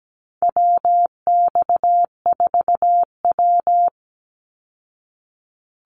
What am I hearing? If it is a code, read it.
Morse code, heard as a keyed tone, about 17 wpm, 700 Hz. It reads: WX4W
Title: Call Signs Repeated 2x - 17wpm